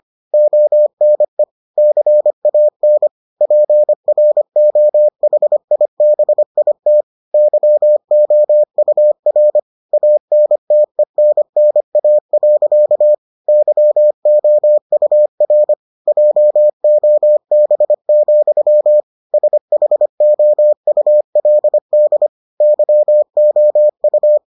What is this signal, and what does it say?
Morse code, heard as a keyed tone, about 25 wpm, 600 Hz. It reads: ONE CAN PROHIBIT YOUR ANTENNA. YOUR JOB, SHOULD YOU